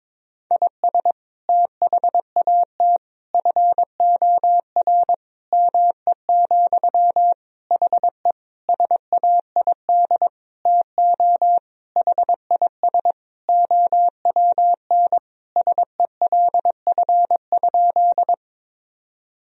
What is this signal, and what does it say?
Morse code, heard as a keyed tone, about 22 wpm, 700 Hz. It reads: IS THAT FOR ME, HE SAID TO HIS OWN SELF?
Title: Sentences from Top 1000 Words - Review Part 1 - 22wpm